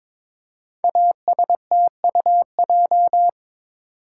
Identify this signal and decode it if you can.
Morse code, heard as a keyed tone, about 23 wpm, 690 Hz. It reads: ASTUJ
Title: Code Groups - 5 Letters - 22wpm